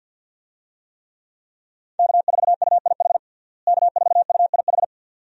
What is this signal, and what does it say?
Morse code, heard as a keyed tone, about 48 wpm, 700 Hz. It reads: K4UIH K4UIH
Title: USA Call Signs - 5 Characters - Repeated 2x - 50wpm